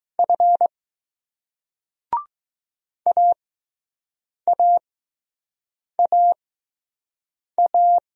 Morse code, heard as a keyed tone, about 19 wpm, spaced as for 6 wpm, 700 Hz. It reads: F AAAA